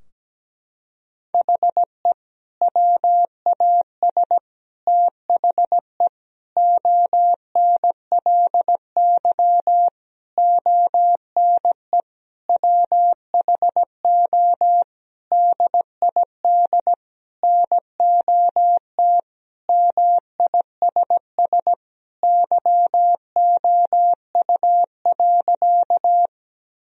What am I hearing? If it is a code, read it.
Morse code, heard as a keyed tone, about 17 wpm, 700 Hz. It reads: HE WAS THE ONLY ONE WHO DID NOT MISS YOU.
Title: Sentences from Top 1000 Words - Review Part 3 - 17wpm